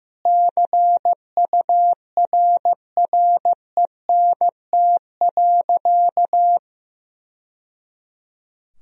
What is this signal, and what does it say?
Morse code, heard as a keyed tone, about 15 wpm, 700 Hz. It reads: CURRENT.
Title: Sentences from Top 1000 Words - Review Part 4 - 15wpm